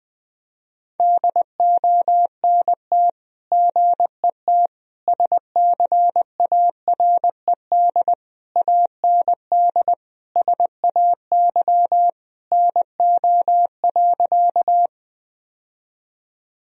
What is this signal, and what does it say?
Morse code, heard as a keyed tone, about 21 wpm, 700 Hz. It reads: DONT GET SCARED AND SAY NO.
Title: Sentences from Top 1000 Words - Review Part 4 - 20wpm